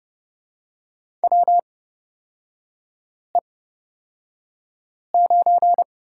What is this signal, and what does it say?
Morse code, heard as a keyed tone, about 30 wpm, spaced as for 5 wpm, 700 Hz. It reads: WE9
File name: Lesson 37e - Letter x - Three Characters 30wpm